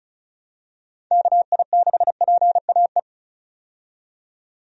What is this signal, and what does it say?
Morse code, heard as a keyed tone, about 34 wpm, 700 Hz. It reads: KI6PAE
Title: USA Call Signs - 6 Characters - 35wpm